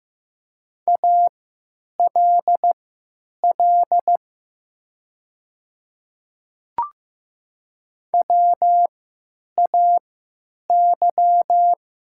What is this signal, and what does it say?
Morse code, heard as a keyed tone, about 15 wpm, spaced as for 9 wpm, 690 Hz. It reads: ALL WAY
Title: 3x Character Spacing - Top 100 Words - 15wpm